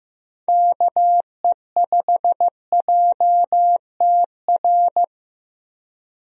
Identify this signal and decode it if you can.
Morse code, heard as a keyed tone, about 15 wpm, 700 Hz. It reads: KE5JTR
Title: USA Call Signs - 6 Characters - Repeated 3x - 15wpm